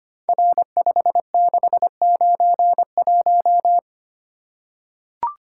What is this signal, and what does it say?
Morse code, heard as a keyed tone, about 25 wpm, 700 Hz. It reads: R5691
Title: Code Groups - 5 Letter:Numbers - 25wpm